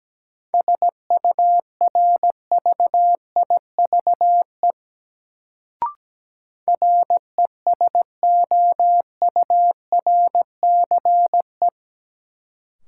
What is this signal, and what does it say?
Morse code, heard as a keyed tone, about 17 wpm, 700 Hz. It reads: SURVIVE RESOURCE
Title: Sentences from Top 1300 Words - New Word Drill - 17wpm